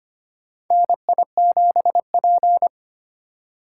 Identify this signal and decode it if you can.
Morse code, heard as a keyed tone, about 25 wpm, 700 Hz. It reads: NI7P
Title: USA Call Signs - 4 Characters - 3x Character Spacing - 25wpm